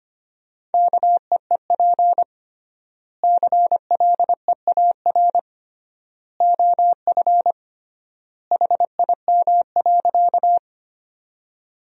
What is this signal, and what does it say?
Morse code, heard as a keyed tone, about 25 wpm, 700 Hz. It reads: KEEP CLEAR OF HIM.
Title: 3x Word-Spacing - Sentences from Top 500 Words - 25wpm